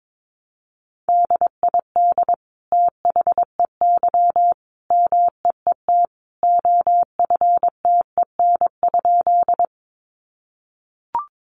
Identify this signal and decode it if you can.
Morse code, heard as a keyed tone, about 22 wpm, 700 Hz. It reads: DID THEY MEET OFTEN?
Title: Sentences from Top 1100 Words 22wpm